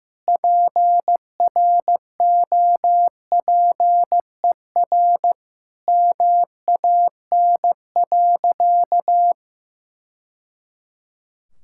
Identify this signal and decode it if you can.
Morse code, heard as a keyed tone, about 15 wpm, 700 Hz. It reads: PROPER MAN.
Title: Sentences from Top 1000 Words - Review Part 5 - 15wpm